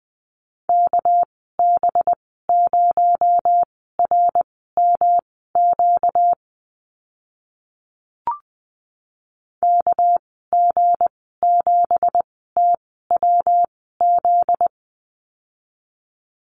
Morse code, heard as a keyed tone, about 20 wpm, spaced as for 15 wpm, 700 Hz. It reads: KB0RMQ KG7TWZ
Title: 245 Lesson 39g - Letter z - Call Signs 20wpm